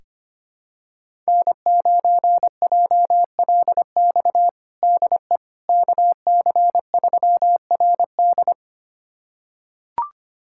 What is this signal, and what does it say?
Morse code, heard as a keyed tone, about 25 wpm, 700 Hz. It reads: N9JLX DE KC3RD
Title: USA Callsign DE Callsign 25wpm